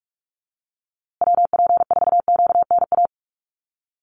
Morse code, heard as a keyed tone, about 44 wpm, 700 Hz. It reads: WP4QNA
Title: USA Call Signs Repeated 3x 45wpm